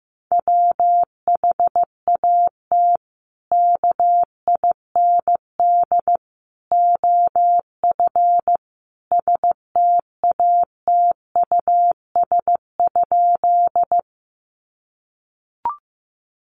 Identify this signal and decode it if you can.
Morse code, heard as a keyed tone, about 15 wpm, 700 Hz. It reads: WHAT KIND OF STATUS?
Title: Sentences from Top 1400 Words 15wpm